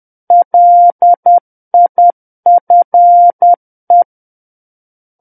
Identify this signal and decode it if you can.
Morse code, heard as a keyed tone, about 10 wpm, 700 Hz. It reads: LIFE
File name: Book One / 01-Chapter 1 - 10 WPM